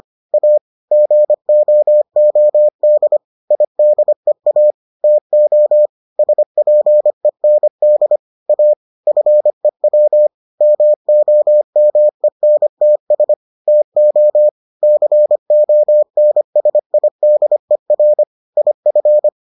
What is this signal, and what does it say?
Morse code, heard as a keyed tone, about 25 wpm, 600 Hz. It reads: A GOOD IDEA TO SPEND A FEW MOMENTS TO CONSIDER IF